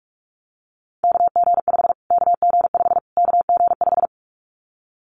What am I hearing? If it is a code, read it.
Morse code, heard as a keyed tone, about 44 wpm, 700 Hz. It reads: KG5 KG5 KG5
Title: Callsign Prefixes for Top 25 DXCC - Repeated 3x - 45wpm